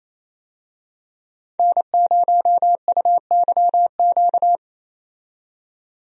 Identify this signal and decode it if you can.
Morse code, heard as a keyed tone, about 28 wpm, 700 Hz. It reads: N0UYQ
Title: USA Call Signs - 5 Characters - 28wpm